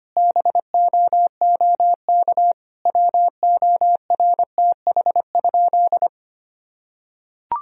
Morse code, heard as a keyed tone, about 25 wpm, 700 Hz. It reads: BOOK WORTH?
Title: Sentences from Top 1000 Words - Review Part 1 - 25wpm